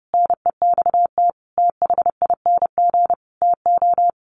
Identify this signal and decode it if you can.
Morse code, heard as a keyed tone, about 30 wpm, 700 Hz. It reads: NEXT THING TO
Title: Sentences from Top 1300 Words 30wpm